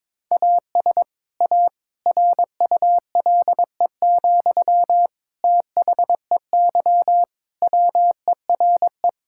Morse code, heard as a keyed tone, about 22 wpm, 700 Hz. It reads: AS A RULE, THEY WERE
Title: Sentences from Top 1000 Words - Review Part 6 - 22wpm